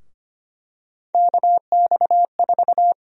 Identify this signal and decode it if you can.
Morse code, heard as a keyed tone, about 25 wpm, 700 Hz. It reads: KX4
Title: USA Call Sign Prefixes - 25wpm